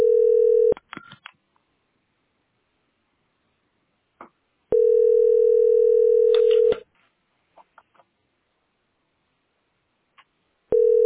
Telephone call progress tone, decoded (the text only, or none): ringback tone